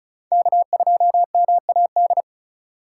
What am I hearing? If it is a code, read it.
Morse code, heard as a keyed tone, about 36 wpm, 700 Hz. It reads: K2MAD
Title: USA Call Signs - 2x Character Spacing - 35wpm